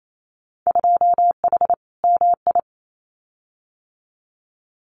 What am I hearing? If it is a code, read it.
Morse code, heard as a keyed tone, about 28 wpm, 700 Hz. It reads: 2H MI
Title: Winter Field Day Exchanges 28wpm